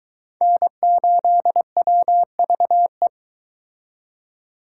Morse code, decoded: N8WVE

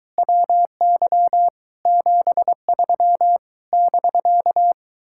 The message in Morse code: WY 73 <BK>

